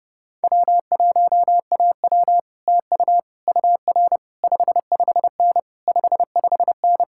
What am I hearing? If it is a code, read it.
Morse code, heard as a keyed tone, about 30 wpm, 700 Hz. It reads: W1AW TU UR 55N 55N